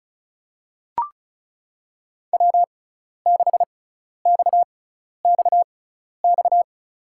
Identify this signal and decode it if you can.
Morse code, heard as a keyed tone, about 36 wpm, spaced as for 12 wpm, 700 Hz. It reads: W6XXX
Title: USA Call Signs - 6x Character Spacing - 35wpm